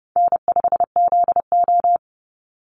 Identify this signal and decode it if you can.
Morse code, heard as a keyed tone, about 32 wpm, 700 Hz. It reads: N5ZO
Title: Call Signs - 30wpm